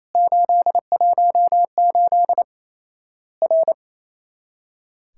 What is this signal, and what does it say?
Morse code, heard as a keyed tone, about 28 wpm, 700 Hz. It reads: 818 R
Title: Worked All Europe DX Contest - QTC Exchange - 28wpm